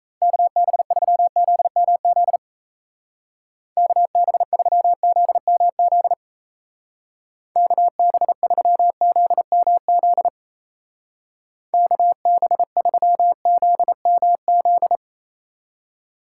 Morse code, heard as a keyed tone, about 34 wpm, 700 Hz. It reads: KB3ZMZ KB3ZMZ KB3ZMZ KB3ZMZ
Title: Speed Racing - USA 2x3 Call Signs 28wpm